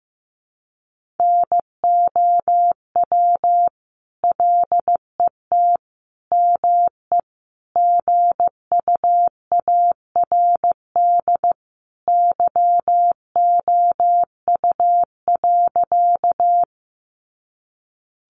Morse code, heard as a keyed tone, about 14 wpm, 700 Hz. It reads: NOW LET ME GUARD YOU.
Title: Sentences from Top 700 Words 15wpm